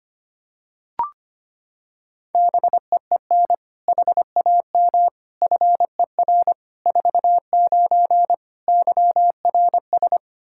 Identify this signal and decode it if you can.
Morse code, heard as a keyed tone, about 25 wpm, 700 Hz. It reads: BEEN HAM FER 49 YRS
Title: Practice QSO - Exchange 3 - More Info - 25wpm